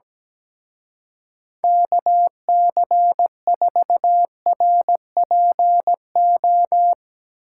Morse code, heard as a keyed tone, about 19 wpm, 700 Hz. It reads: KC4RPO
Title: Speed Racing - USA 2x3 Call Signs 17wpm